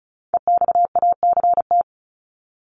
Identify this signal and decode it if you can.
Morse code, heard as a keyed tone, about 34 wpm, 700 Hz. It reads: EXACT
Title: LY Suffix - Top 100 Words with Base Words 35wpm